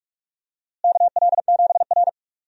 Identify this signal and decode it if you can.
Morse code, heard as a keyed tone, about 46 wpm, 700 Hz. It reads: KL7R
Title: DXCC-100 Call Signs - 4 Characters - Repeated 2x - 45wpm